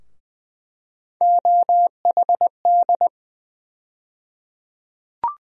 Morse code, heard as a keyed tone, about 20 wpm, 700 Hz. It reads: OHD
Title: USA Call Sign Postfixes - 3 Letters 20wpm